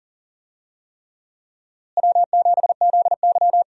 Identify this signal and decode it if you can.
Morse code, heard as a keyed tone, about 37 wpm, 700 Hz. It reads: W7ZY
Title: USA Call Signs - 4 Characters - Repeated 3x - 40wpm